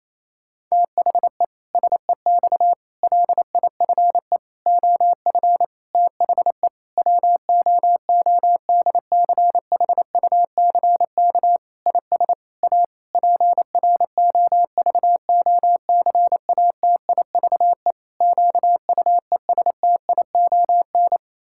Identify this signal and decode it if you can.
Morse code, heard as a keyed tone, about 28 wpm, 700 Hz. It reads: THE SEX LIFE OF THE WOODCHUCK IS A PROVOCATIVE QUESTION
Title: Pangram Challenge v2 - 28wpm